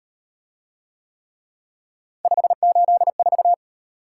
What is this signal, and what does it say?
Morse code, heard as a keyed tone, about 38 wpm, 700 Hz. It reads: 584